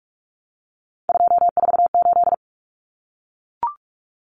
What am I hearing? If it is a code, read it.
Morse code, heard as a keyed tone, about 44 wpm, 700 Hz. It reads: 248
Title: Numbers - 3-Digit 45wpm